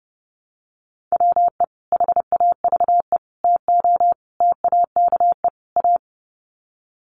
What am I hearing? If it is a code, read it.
Morse code, heard as a keyed tone, about 30 wpm, 700 Hz. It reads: WE HAVE TO TAKE A
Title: Sets of 5 Words Encore 30wpm